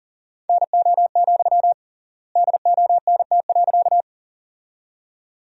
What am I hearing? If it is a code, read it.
Morse code, heard as a keyed tone, about 38 wpm, 700 Hz. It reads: NO, DONT.
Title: 3x Word-Spacing - Sentences from Top 300 Words - 40wpm